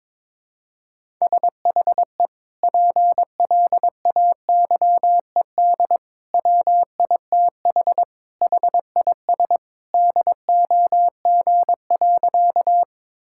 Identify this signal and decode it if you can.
Morse code, heard as a keyed tone, about 22 wpm, 700 Hz. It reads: SHE PLAYED WITH HIS DOG.
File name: practice / 22wpm / Sentences with 2-Word Phrases - 22wpm